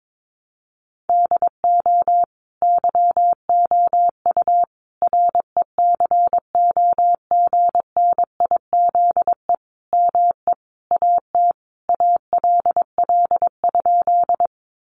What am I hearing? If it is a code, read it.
Morse code, heard as a keyed tone, about 22 wpm, 700 Hz. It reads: DO YOU RECOGNIZE ME AT ALL?
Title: Sentences from Top 1400 Words 22wpm